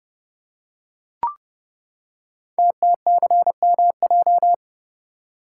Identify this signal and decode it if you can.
Morse code, heard as a keyed tone, about 30 wpm, 700 Hz. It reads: TTCMJ
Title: Code Groups - 5 Letters - 30wpm